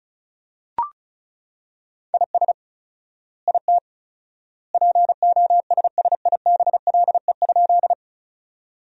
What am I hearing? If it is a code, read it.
Morse code, heard as a keyed tone, about 36 wpm, 700 Hz. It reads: IS IT POSSIBLE?
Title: 4x Word-Spacing - Sentences from Top 500 Words - 35wpm